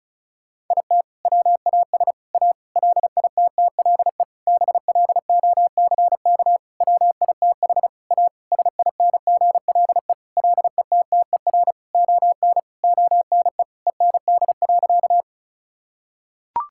Morse code, heard as a keyed tone, about 36 wpm, 700 Hz. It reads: IT WAS A LITTLE BLOCK WITH A SINGLE LETTER ON ONE END.
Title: Sentences from Top 1600 Words 35wpm